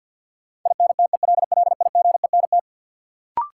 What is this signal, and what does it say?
Morse code, heard as a keyed tone, about 47 wpm, 700 Hz. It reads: INTELLIGENT
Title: Sentences from Top 1000 Words - Review Part 7 - New Word Drill - 50wpm